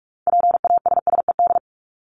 Morse code, heard as a keyed tone, about 44 wpm, 700 Hz. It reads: PASSED
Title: ED Suffix - Top 100 Words with Base Words 45wpm